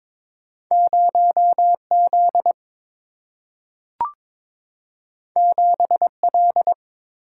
Morse code, heard as a keyed tone, about 22 wpm, 700 Hz. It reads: 0 Z 7 L
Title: Single-Letter Single-Number 22wpm